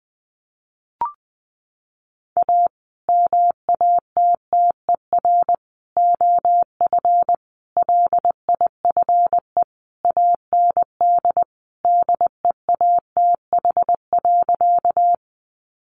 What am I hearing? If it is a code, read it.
Morse code, heard as a keyed tone, about 20 wpm, 700 Hz. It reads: A MATTER OF LIFE AND DEATH.
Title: Sentences from Top 1100 Words 20wpm